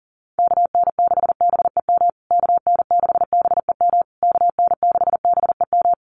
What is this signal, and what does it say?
Morse code, heard as a keyed tone, about 40 wpm, 700 Hz. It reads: KN6BEM KN6BEM KN6BEM
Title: USA Call Signs Repeated 3x 40wpm